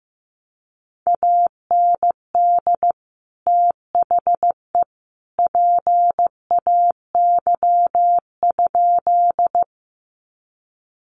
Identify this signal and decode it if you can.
Morse code, heard as a keyed tone, about 15 wpm, 700 Hz. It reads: AND THE PAY?
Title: Sentences from Top 700 Words 15wpm